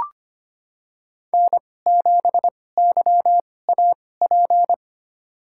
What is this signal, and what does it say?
Morse code, heard as a keyed tone, about 25 wpm, spaced as for 18 wpm, 700 Hz. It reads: N7YAP
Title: USA Call Signs - 5 Characters - 2x Character Spacing - 25wpm